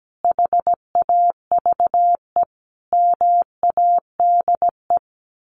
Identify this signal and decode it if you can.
Morse code, heard as a keyed tone, about 17 wpm, 700 Hz. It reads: HAVE MADE
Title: Sentences from Top 1400 Words 17wpm